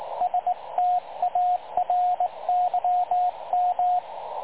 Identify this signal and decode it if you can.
Morse code, heard as a keyed tone, about 21 wpm, 720 Hz, with radio noise. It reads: STARYM
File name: dlazaawansowanych / nagraniazpasma / websdr recording 3565.0kHz